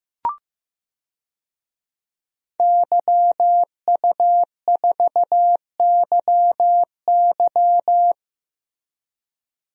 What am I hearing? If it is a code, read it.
Morse code, heard as a keyed tone, about 15 wpm, 700 Hz. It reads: YU4YY
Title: DXCC-100 Call Signs - 5 Characters - Repeated 3x - 15wpm